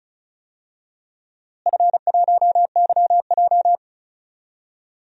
Morse code, heard as a keyed tone, about 36 wpm, 700 Hz. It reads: F1YJ
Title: DXCC-100 Call Signs - 4 Characters - Repeated 2x - 35wpm